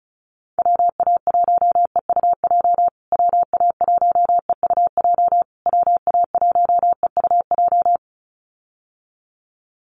Morse code, heard as a keyed tone, about 34 wpm, 700 Hz. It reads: WA1EUJ WA1EUJ WA1EUJ
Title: USA Call Signs Repeated 3x 35wpm